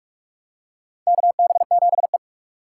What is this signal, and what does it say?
Morse code, heard as a keyed tone, about 46 wpm, 700 Hz. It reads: KB7E
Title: USA Call Signs - 4 Characters - 2x Character Spacing - 45wpm